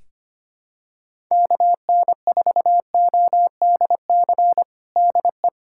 Morse code, heard as a keyed tone, about 25 wpm, 700 Hz. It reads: KN4ODC DE
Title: USA Callsign DE Callsign 25wpm